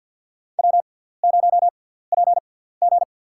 Morse code, heard as a keyed tone, about 53 wpm, spaced as for 17 wpm, 700 Hz. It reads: W0PG
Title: USA Call Signs - 4 Characters - 6x Character Spacing - 50wpm